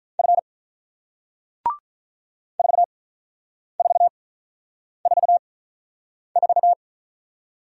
Morse code, heard as a keyed tone, about 48 wpm, spaced as for 9 wpm, 700 Hz. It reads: F 4444